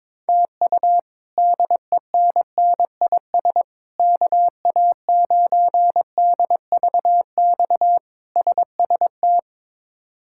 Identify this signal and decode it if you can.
Morse code, heard as a keyed tone, about 22 wpm, 700 Hz. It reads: TU DENNIS KA9DVX SST
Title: K1USN SST QSO Practice - 22wpm